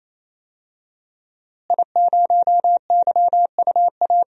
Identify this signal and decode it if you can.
Morse code, heard as a keyed tone, about 28 wpm, 700 Hz. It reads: I0YUA
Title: DXCC-100 Call Signs - 5 Characters - Repeated 3x - 28wpm